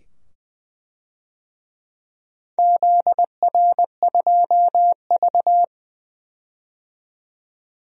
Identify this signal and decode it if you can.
Morse code, heard as a keyed tone, about 20 wpm, 700 Hz. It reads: ZR2V
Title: DXCC-100 Call Signs - 4 Characters - 20wpm